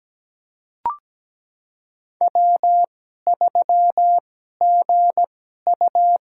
Morse code, heard as a keyed tone, about 17 wpm, spaced as for 12 wpm, 700 Hz. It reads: W3GU
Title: USA Call Signs - 2x Character Spacing - 17wpm